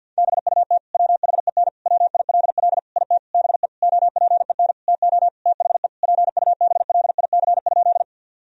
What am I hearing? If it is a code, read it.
Morse code, heard as a keyed tone, about 50 wpm, 700 Hz. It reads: BUT WHEN WILL IT BE OPEN TO THE PUBLIC?
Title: Sentences from Top 1000 Words - Review Part 3 - 50wpm